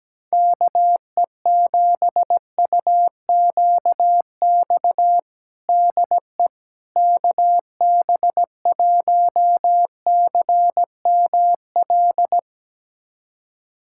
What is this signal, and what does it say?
Morse code, heard as a keyed tone, about 17 wpm, 700 Hz. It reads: KE7UQX DE KB1CML
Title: USA Callsign DE Callsign 17wpm